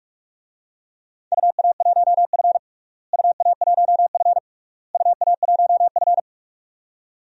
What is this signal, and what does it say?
Morse code, heard as a keyed tone, about 47 wpm, 700 Hz. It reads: UA1F UA1F UA1F
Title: DXCC-100 Call Signs - 4 Characters - Repeated 3x - 45wpm